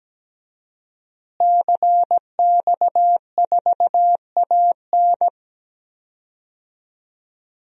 Morse code, heard as a keyed tone, about 17 wpm, 700 Hz. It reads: CX4AN
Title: DXCC-100 Call Signs - 5 Characters - Repeated 2x - 17wpm